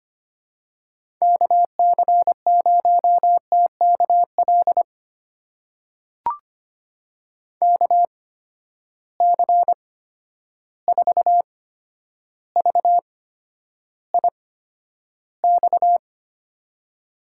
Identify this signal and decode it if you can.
Morse code, heard as a keyed tone, about 25 wpm, spaced as for 11 wpm, 700 Hz. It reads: KC0TKL KC4VIX
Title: USA Call Signs - 6 Characters - 8x Character Spacing - 25wpm